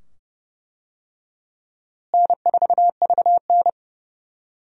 Morse code, heard as a keyed tone, about 32 wpm, 700 Hz. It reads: N4VN
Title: USA Call Signs - 4 Characters - Repeated 2x - 30wpm